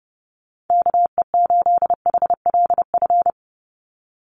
Morse code, heard as a keyed tone, about 30 wpm, 700 Hz. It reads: KE8HLF
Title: USA Call Signs 30wpm